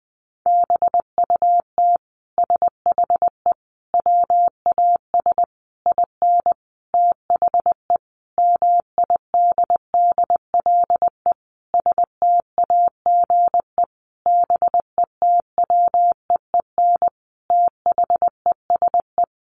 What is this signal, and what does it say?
Morse code, heard as a keyed tone, about 20 wpm, 700 Hz. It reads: BUT SHE WAS IN THE MIDDLE STAGE BETWEEN THESE
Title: Sentences from Top 1200 Words 20wpm